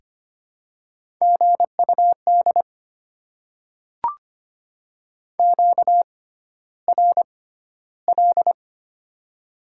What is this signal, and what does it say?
Morse code, heard as a keyed tone, about 25 wpm, spaced as for 13 wpm, 700 Hz. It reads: GUD QRL
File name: Top 20 QSO Elements - 6x Character Spacing - 25wpm